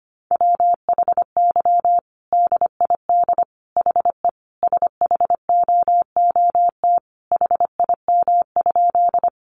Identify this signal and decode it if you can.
Morse code, heard as a keyed tone, about 25 wpm, 700 Hz. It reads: WHY DID HE SHOOT HIM?
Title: Sentences from Top 1200 Words 25wpm